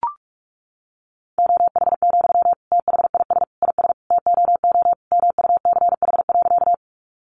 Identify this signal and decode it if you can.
Morse code, heard as a keyed tone, about 44 wpm, 700 Hz. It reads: OH, THIS IS TOO MUCH.
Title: Sentences from Top 500 Words 45wpm